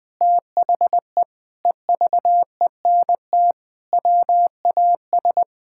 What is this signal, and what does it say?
Morse code, heard as a keyed tone, about 21 wpm, 700 Hz. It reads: THE EVENT WAS
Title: Sentences with 7-Word Phrases - Spoken Beginning - 20wpm